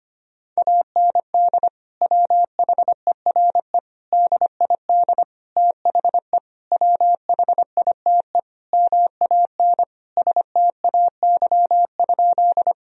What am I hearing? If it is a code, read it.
Morse code, heard as a keyed tone, about 25 wpm, 700 Hz. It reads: AND WHERE DID THE WHITE MAN STAY?
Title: Sentences from Top 1000 Words - Review Part 1 - 25wpm